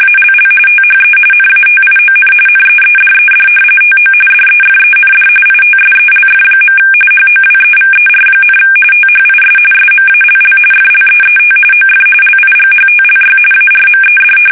RTTY, about 45.45 baud. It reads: ALL THE NATION NEEDED TO DEFEND ITSELF. HOWEVER, THIS SMALL FLEET OF LIGHTLY ARMED
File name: fsk850 / usnavy1